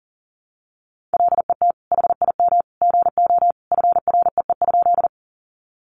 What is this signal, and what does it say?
Morse code, heard as a keyed tone, about 40 wpm, 700 Hz. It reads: LET HIM GO FREE?
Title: Sentences from Top 500 Words 40wpm